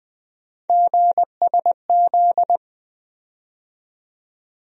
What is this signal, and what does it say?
Morse code, heard as a keyed tone, about 20 wpm, 700 Hz. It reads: GSZ
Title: USA Call Sign Postfixes - 3 Letters 20wpm